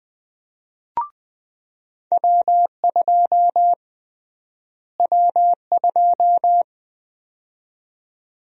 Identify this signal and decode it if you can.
Morse code, heard as a keyed tone, about 20 wpm, 700 Hz. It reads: W2 W2
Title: USA Call Sign Prefixes - Repeated 2x - 20wpm